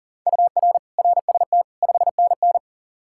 Morse code, heard as a keyed tone, about 38 wpm, 700 Hz. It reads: UR RST 5NN
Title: Practice QSO - Exchange 1 - Beginning - 40wpm